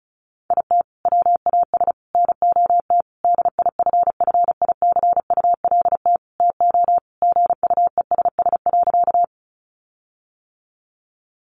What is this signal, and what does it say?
Morse code, heard as a keyed tone, about 34 wpm, 700 Hz. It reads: IT WAS NOT DIFFICULT TO GUESS.
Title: Sentences from Top 1500 Words 35wpm